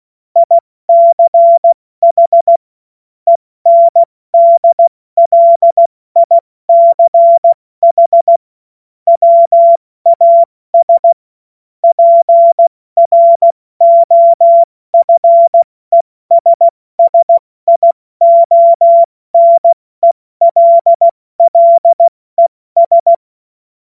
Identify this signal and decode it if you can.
Morse code, heard as a keyed tone, about 16 wpm, 680 Hz. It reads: ICH ENDLICH WAS PROFESSIONELLES
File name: Morsekurs / Witze / Witz 27. April 2021, 70 F80